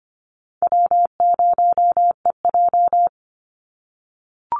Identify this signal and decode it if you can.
Morse code, heard as a keyed tone, about 25 wpm, 700 Hz. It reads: W0EJ
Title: Easy Call Signs - 25wpm